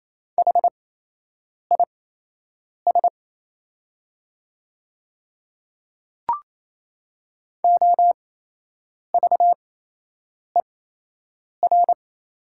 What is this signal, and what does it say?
Morse code, heard as a keyed tone, about 27 wpm, spaced as for 8 wpm, 690 Hz. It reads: HIS OVER
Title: 8x Character Spacing - Top 200 Words - 28wpm